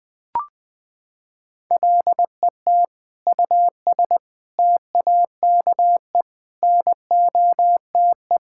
Morse code, heard as a keyed tone, about 21 wpm, 700 Hz. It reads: LET US TAKE NOTE
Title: Sentences from Top 1600 Words 20wpm